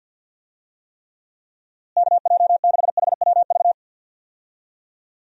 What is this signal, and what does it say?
Morse code, heard as a keyed tone, about 53 wpm, 700 Hz. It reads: KJ6HWV
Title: USA Call Signs - 6 Characters - Repeated 2x - 50wpm